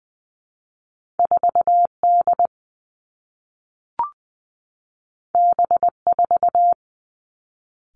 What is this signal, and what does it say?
Morse code, heard as a keyed tone, about 20 wpm, 700 Hz. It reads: 4 D B 4
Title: Two Challenging Letter-Number Combinations 20wpm